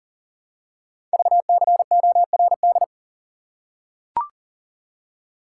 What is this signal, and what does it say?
Morse code, heard as a keyed tone, about 38 wpm, 700 Hz. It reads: VCORD
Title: Code Groups - 5 Letters - 40wpm